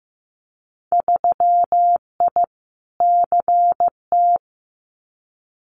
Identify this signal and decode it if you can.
Morse code, heard as a keyed tone, about 15 wpm, 700 Hz. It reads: 3I CT
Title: Winter Field Day Exchanges 15wpm